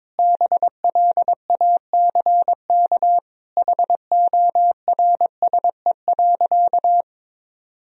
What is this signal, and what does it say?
Morse code, heard as a keyed tone, about 22 wpm, 700 Hz. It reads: BLACK HORSE.
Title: Sentences from Top 1000 Words - Review Part 2 - 22wpm